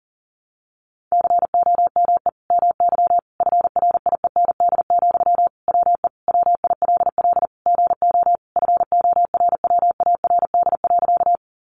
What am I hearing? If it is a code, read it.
Morse code, heard as a keyed tone, about 40 wpm, 700 Hz. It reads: COME MY FRIEND, WE WILL GO FORWARD.
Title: Sentences from Top 1300 Words 40wpm